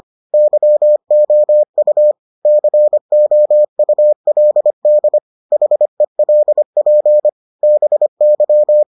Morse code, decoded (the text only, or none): YOU COULD HELP BY